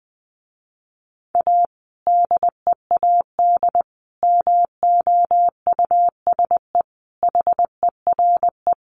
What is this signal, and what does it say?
Morse code, heard as a keyed tone, about 20 wpm, 700 Hz. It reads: A DEAD MOUSE HERE
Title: A Sound Of Thunder 20wpm